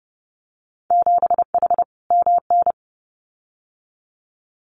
Morse code, heard as a keyed tone, about 30 wpm, 700 Hz. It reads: 7H MN
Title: Winter Field Day Exchanges Encore 30wpm